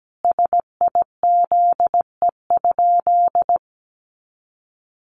Morse code, heard as a keyed tone, about 17 wpm, 700 Hz. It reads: SIZE?
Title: Sentences from Top 500 Words 17wpm